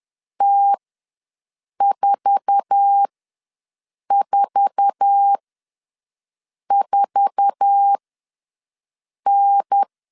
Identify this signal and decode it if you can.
Morse code, heard as a keyed tone, about 10 wpm, spaced as for 5 wpm, 800 Hz. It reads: T444N
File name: LEZIONI / 8 YJ45 / 8 YJ4 ws 10wpm 4020